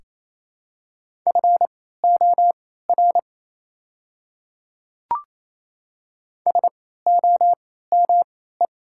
Morse code, heard as a keyed tone, about 28 wpm, spaced as for 16 wpm, 700 Hz. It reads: FOR SOME
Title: 3x Character Spacing - Top 100 Words - 28wpm